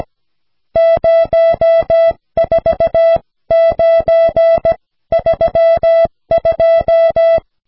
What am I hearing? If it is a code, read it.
Morse code, heard as a keyed tone, about 17 wpm, spaced as for 14 wpm, 660 Hz. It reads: 04932
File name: cw / snelheid 4